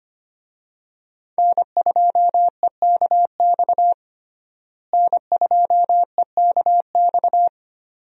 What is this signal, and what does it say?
Morse code, heard as a keyed tone, about 25 wpm, 700 Hz. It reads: N2EKX N2EKX
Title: USA Call Signs - 5 Characters - Repeated 2x - 25wpm